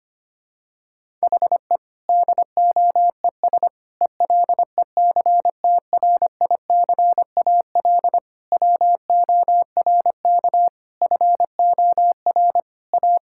Morse code, heard as a keyed tone, about 25 wpm, 700 Hz. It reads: HE DOES ELECTRICAL WORK FOR A
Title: Sentences from Top 1000 Words - Review Part 6 - 25wpm